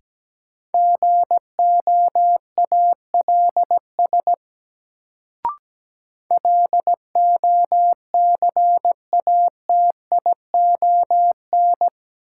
Morse code, heard as a keyed tone, about 17 wpm, 700 Hz. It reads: GOALS LOCATION
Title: Sentences from Top 1000 Words - Review Part 6 - New Word Drill - 17wpm